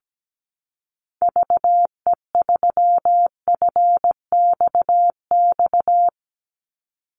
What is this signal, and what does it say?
Morse code, heard as a keyed tone, about 17 wpm, 700 Hz. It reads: VE3FXX
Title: Call Signs Repeated 3x - 17wpm